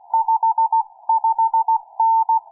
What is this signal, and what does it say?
Morse code, heard as a keyed tone, about 16 wpm, 900 Hz. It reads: 55N
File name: cw forum / vk6abl 1826 05avril08 00h22loc